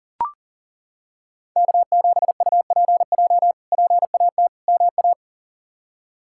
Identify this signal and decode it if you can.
Morse code, heard as a keyed tone, about 37 wpm, 700 Hz. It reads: K7UPJ PAT MA
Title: K1USN SST QSO Practice - 40wpm